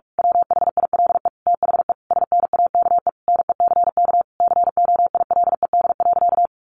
Morse code, heard as a keyed tone, about 44 wpm, 700 Hz. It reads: WHILE THE SNAKE NECK COILED.
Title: A Sound Of Thunder 45wpm